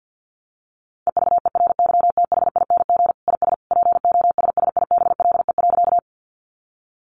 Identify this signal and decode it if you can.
Morse code, heard as a keyed tone, about 48 wpm, 700 Hz. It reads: EVERYTHING IS POSSIBLE.
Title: Sentences from Top 700 Words 50wpm